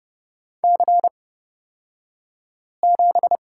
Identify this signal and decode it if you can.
Morse code, heard as a keyed tone, about 30 wpm, spaced as for 5 wpm, 700 Hz. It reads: C7
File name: Lesson 38d - Letter q - Two Characters 30wpm